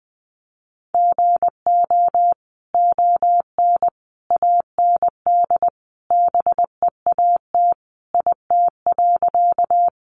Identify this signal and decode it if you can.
Morse code, heard as a keyed tone, about 20 wpm, 700 Hz. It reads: GO ON AND BEAT IT.
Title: Sentences from Top 1300 Words 20wpm